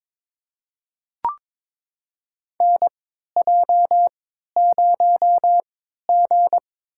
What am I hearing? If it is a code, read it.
Morse code, heard as a keyed tone, about 21 wpm, spaced as for 12 wpm, 690 Hz. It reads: NJ0G
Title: USA Call Signs - 4 Characters - 3x Character Spacing - 22wpm